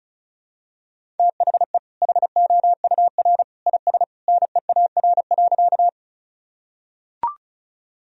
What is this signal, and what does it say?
Morse code, heard as a keyed tone, about 34 wpm, 700 Hz. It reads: THE HOUR IS NEAR.
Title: Sentences from Top 1000 Words - Review Part 1 - 35wpm